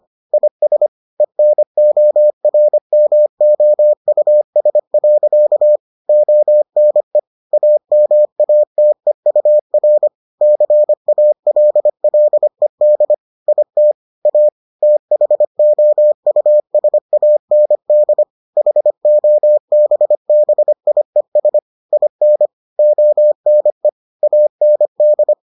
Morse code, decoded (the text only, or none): IS ENORMOUS. ONE AMATEUR CALLED IT A THOUSAND HOBBIES IN ONE AND